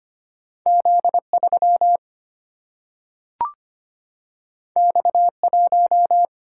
Morse code, heard as a keyed tone, about 25 wpm, 700 Hz. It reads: Z3 X1
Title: Single-Letter Single-Number 25wpm